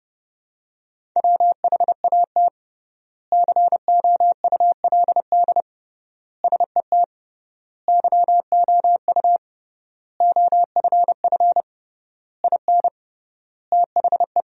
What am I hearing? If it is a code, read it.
Morse code, heard as a keyed tone, about 32 wpm, 700 Hz. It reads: WHAT COULD SET YOU OFF IN THE
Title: 3x Word-Spacing - Sentences from Top 200 Words - 30wpm